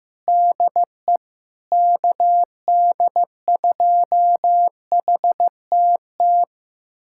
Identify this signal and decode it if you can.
Morse code, heard as a keyed tone, about 15 wpm, 700 Hz. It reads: DE KD2HTT
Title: USA Callsign DE Callsign - 6 Characters - 15wpm